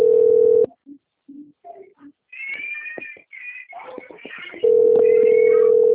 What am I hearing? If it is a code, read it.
Ringback tone (North America), 440 + 480 Hz.
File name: out- - - - - .58538